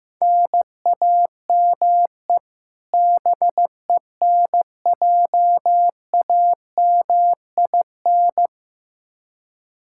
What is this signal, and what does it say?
Morse code, heard as a keyed tone, about 15 wpm, 700 Hz. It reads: NAME BENJAMIN
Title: K1USN SST QSO Practice 15wpm